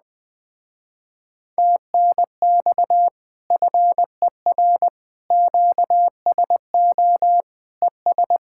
Morse code, heard as a keyed tone, about 20 wpm, 700 Hz. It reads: TNX FER QSO ES
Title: Speed-Racing Common QSO Phrases - 17wpm